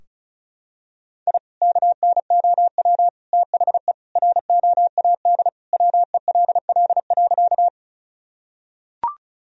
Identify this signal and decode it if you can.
Morse code, heard as a keyed tone, about 34 wpm, 700 Hz. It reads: I KNOW THE ROAD WELL.
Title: Sentences from Top 1000 Words - Review Part 2 - 35wpm